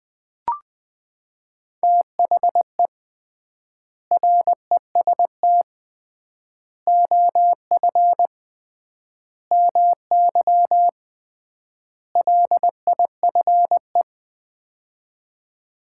Morse code, heard as a keyed tone, about 20 wpm, 700 Hz. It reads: THE REST OF MY LIFE
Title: 3x Word-Spacing - Sets of 5 Words - 20wpm